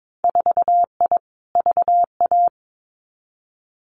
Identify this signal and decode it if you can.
Morse code, heard as a keyed tone, about 22 wpm, 700 Hz. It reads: 4I VA
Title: Winter Field Day Exchanges 22wpm